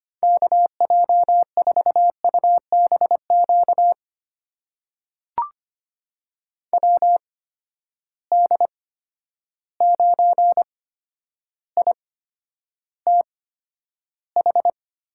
Morse code, heard as a keyed tone, about 25 wpm, spaced as for 11 wpm, 700 Hz. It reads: KJ4UBQ WD9ITH
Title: USA Call Signs - 6 Characters - 8x Character Spacing - 25wpm